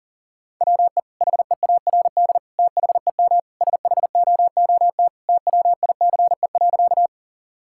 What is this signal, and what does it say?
Morse code, heard as a keyed tone, about 40 wpm, 700 Hz. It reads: WE HEARD THEM SHOOT TWICE.
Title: Sentences from Top 1600 Words 40wpm